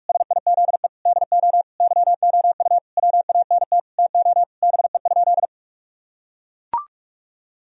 Morse code, heard as a keyed tone, about 41 wpm, 700 Hz. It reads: SIZE DO YOU WANT TO BE?
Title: Sentences from Top 1000 Words - Review Part 2 - 45wpm